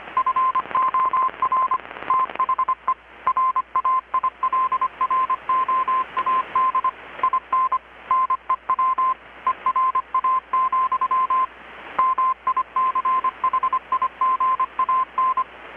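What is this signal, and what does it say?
Morse code, heard as a keyed tone, about 25 wpm, 1040 Hz. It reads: FOR THE RAILROAD IN NEW ERA, MICHIGAN
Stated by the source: Morse from KPH's Night of Nights XX maritime radiotelegraphy event, received in Chiapas, Mexico about 3,800 km away on a Kenwood R-600 with a grounded 10 m long-wire antenna.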